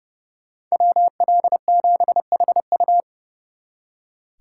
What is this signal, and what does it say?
Morse code, heard as a keyed tone, about 35 wpm, 700 Hz. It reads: WL7HU
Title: Speed Racing - USA Call Signs 30wpm